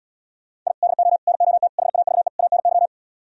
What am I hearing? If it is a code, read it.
Morse code, heard as a keyed tone, about 77 wpm, 700 Hz. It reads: I DO NOT HAVE ANY